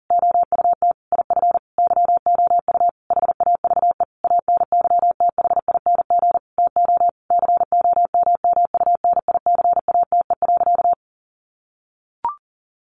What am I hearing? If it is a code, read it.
Morse code, heard as a keyed tone, about 40 wpm, 700 Hz. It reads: OUT IF YOU HAVE ANYTHING TO COMMUNICATE.
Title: Sentences from Top 1300 Words 40wpm